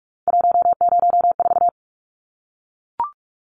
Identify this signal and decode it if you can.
Morse code, heard as a keyed tone, about 44 wpm, 700 Hz. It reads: 104
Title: Numbers - 3-Digit 45wpm